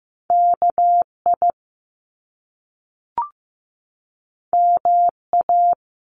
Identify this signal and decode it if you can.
Morse code, heard as a keyed tone, about 15 wpm, 700 Hz. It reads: KI MA
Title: Top Two Letter Combinations in Words 15wpm